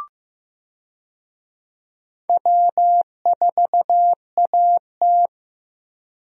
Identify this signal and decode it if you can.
Morse code, heard as a keyed tone, about 15 wpm, 700 Hz. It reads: W4AT
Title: USA Call Signs - 4 Characters - Repeated 2x - 15wpm